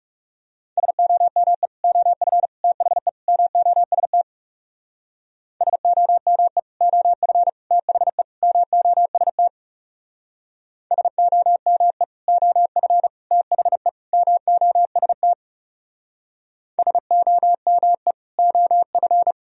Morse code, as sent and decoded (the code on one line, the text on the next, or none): ... --- -- . / --- ..-. / - .... / -- --- ... - / ... --- -- . / --- ..-. / - .... / -- --- ... - / ... --- -- . / --- ..-. / - .... / -- --- ... - / ... --- -- . / --- ..-.
SOME OF THE MOST SOME OF THE MOST SOME OF THE MOST SOME OF